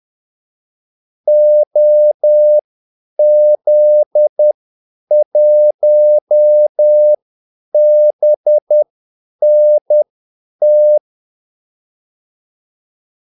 Morse code, heard as a keyed tone, about 10 wpm, spaced as for 8 wpm, 600 Hz. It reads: OZ1BNT
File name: Callsigns-10-8wpm